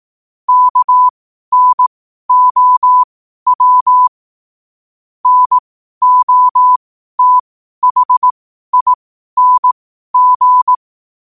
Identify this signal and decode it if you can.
Morse code, heard as a keyed tone, about 18 wpm, spaced as for 12 wpm, 1000 Hz. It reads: KNOW NOTHING
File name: qotdfiles / 12wpm